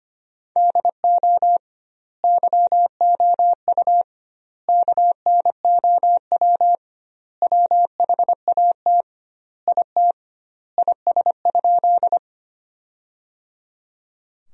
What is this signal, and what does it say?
Morse code, heard as a keyed tone, about 25 wpm, 700 Hz. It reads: DO YOU KNOW WHAT IT IS?
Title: 2x Word-Spacing - Sentences from Top 100 Words - 25wpm